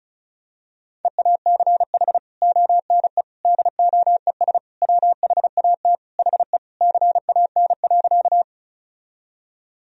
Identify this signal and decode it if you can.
Morse code, heard as a keyed tone, about 34 wpm, 700 Hz. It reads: EACH ONE DOES WHAT HE CAN.
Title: Sentences from Top 1000 Words - Review Part 1 - 35wpm